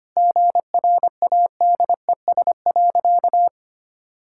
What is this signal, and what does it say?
Morse code, heard as a keyed tone, about 25 wpm, 700 Hz. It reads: GRADES.
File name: practice / 25wpm / Sentences with 3-Word Phrases - 25wpm